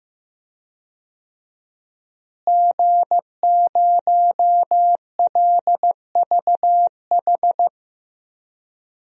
Morse code, heard as a keyed tone, about 15 wpm, 700 Hz. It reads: G0LVH
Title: DXCC-100 Call Signs - 5 Characters - Repeated 2x - 15wpm